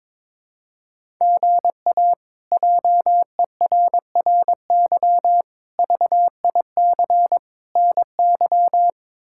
Morse code, heard as a keyed tone, about 22 wpm, 700 Hz. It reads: GA JERRY VIC NY
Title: K1USN SST QSO Practice - 22wpm